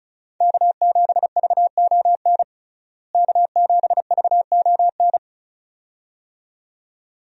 Morse code, heard as a keyed tone, about 36 wpm, 700 Hz. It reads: K7VON K7VON
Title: USA Call Signs - 5 Characters - Repeated 2x - 35wpm